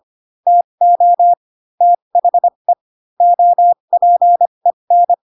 Morse code, heard as a keyed tone, about 25 wpm, spaced as for 22 wpm, 700 Hz. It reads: TO THE OPEN
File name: Pinocchio / Pinocchio ch3 18 25 ik1hge